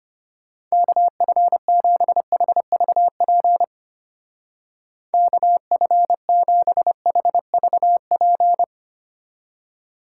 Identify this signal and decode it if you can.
Morse code, heard as a keyed tone, about 29 wpm, 700 Hz. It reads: KF7HVP KF7HVP
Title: Speed Racing - USA 2x3 Call Signs 25wpm